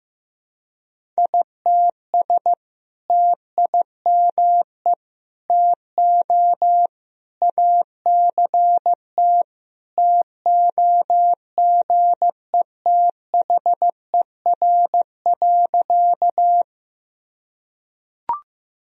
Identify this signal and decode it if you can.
Morse code, heard as a keyed tone, about 15 wpm, 700 Hz. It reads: ITS TIME TO ACT TOGETHER.
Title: Sentences from Top 1000 Words - Review Part 1 - 15wpm